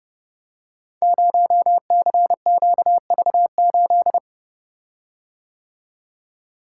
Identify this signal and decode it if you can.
Morse code, heard as a keyed tone, about 29 wpm, 700 Hz. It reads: 0CQV8
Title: Code Groups - 5 Letter:Numbers - 30wpm